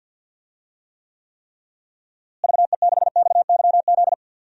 Morse code, heard as a keyed tone, about 50 wpm, 700 Hz. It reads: VE6XYZ
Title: DXCC-100 Call Signs - 6 Characters - Repeated 3x - 50wpm